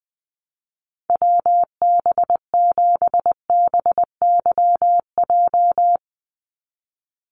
Morse code, heard as a keyed tone, about 19 wpm, 700 Hz. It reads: WB7BYJ
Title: USA Call Signs 20wpm